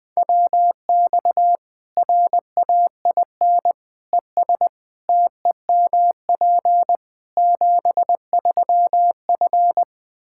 Morse code, heard as a keyed tone, about 21 wpm, 700 Hz. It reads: WX RAIN ES TEMP 73F
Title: Practice QSO - Exchange 2 - Info - 20wpm